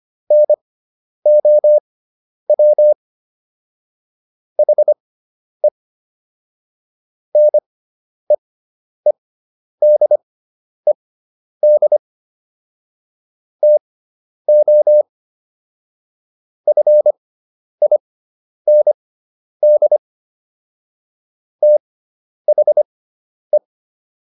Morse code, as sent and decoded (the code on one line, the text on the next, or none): -. --- .-- / .... / -. . -.. -.. / - --- / ..-. .. -. -.. / - .... .
NOW HE NEEDED TO FIND THE